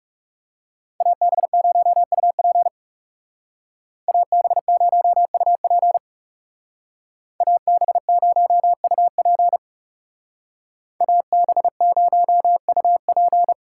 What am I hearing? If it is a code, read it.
Morse code, heard as a keyed tone, about 37 wpm, 700 Hz. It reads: AB0UP AB0UP AB0UP AB0UP
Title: Speed Racing - USA Call Signs 30wpm